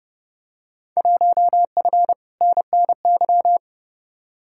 Morse code, heard as a keyed tone, about 32 wpm, 700 Hz. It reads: 1F NNY
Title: Field Day Exchanges - 30wpm